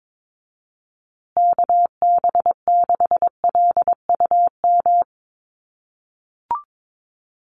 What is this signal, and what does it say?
Morse code, heard as a keyed tone, about 22 wpm, 700 Hz. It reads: KB6LUM
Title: USA Call Signs Repeated 2x 22wpm